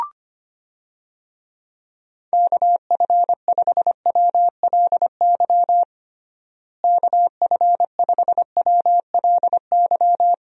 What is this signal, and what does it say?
Morse code, heard as a keyed tone, about 25 wpm, 700 Hz. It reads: KF5WLY KF5WLY
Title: USA Call Signs - 6 Characters - Repeated 3x - 25wpm